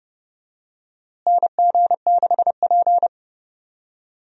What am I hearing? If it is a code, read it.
Morse code, heard as a keyed tone, about 32 wpm, 700 Hz. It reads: NG6P
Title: USA Call Signs - 4 Characters - Repeated 3x - 30wpm